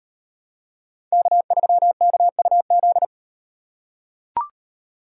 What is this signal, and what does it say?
Morse code, heard as a keyed tone, about 38 wpm, 700 Hz. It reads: K3KUZ